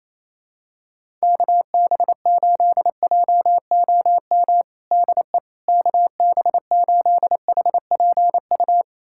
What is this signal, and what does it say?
Morse code, heard as a keyed tone, about 28 wpm, 700 Hz. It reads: KB8JOM DE KB8HPU
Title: USA Callsign DE Callsign - 6 Characters - 28wpm